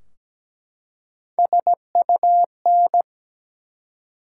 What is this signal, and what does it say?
Morse code, heard as a keyed tone, about 17 wpm, 700 Hz. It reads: SUN
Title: Three Letter Words - 17wpm